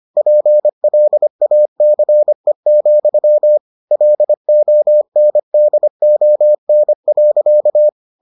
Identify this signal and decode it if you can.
Morse code, heard as a keyed tone, about 25 wpm, 600 Hz. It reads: PLACE, LONDON.